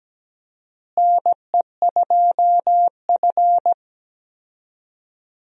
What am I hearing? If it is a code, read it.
Morse code, heard as a keyed tone, about 17 wpm, 700 Hz. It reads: NE2F
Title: USA Call Signs - 4 Characters - Repeated 3x - 17wpm